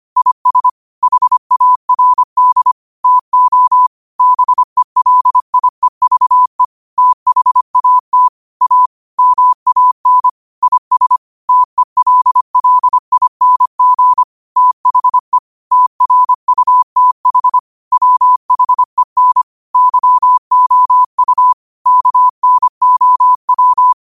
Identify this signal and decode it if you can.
Morse code, heard as a keyed tone, about 25 wpm, 1000 Hz. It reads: IS HARD TO BELIEVE THAT A MAN IS TELLING THE TRUTH WHEN YOU KNOW